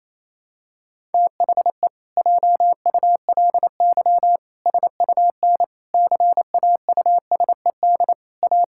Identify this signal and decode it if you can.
Morse code, heard as a keyed tone, about 27 wpm, 700 Hz. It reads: THE JULY SUN CAUSED A
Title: Pangram Challenge v2 - 28wpm